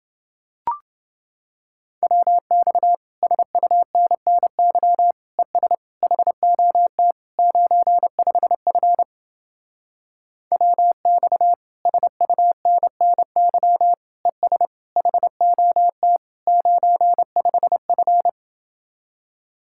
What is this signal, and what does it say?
Morse code, heard as a keyed tone, about 30 wpm, 700 Hz. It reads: WX SUNNY ES HOT 95F WX SUNNY ES HOT 95F